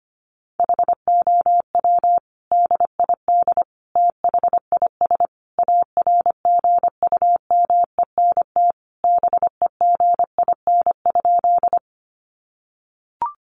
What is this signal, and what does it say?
Morse code, heard as a keyed tone, about 25 wpm, 700 Hz. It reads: HOW DID THIS ARGUMENT BEGIN?
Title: Sentences from Top 1400 Words 25wpm